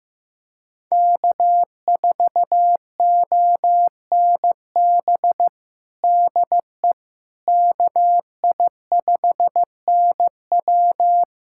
Morse code, heard as a keyed tone, about 15 wpm, 700 Hz. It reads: K4ONB DE KI5NW
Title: USA Callsign DE Callsign 15wpm